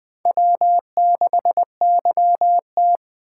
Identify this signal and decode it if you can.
Morse code, heard as a keyed tone, about 20 wpm, 700 Hz. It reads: W6YT
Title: Speed Racing - USA 1x2 Call Signs 17wpm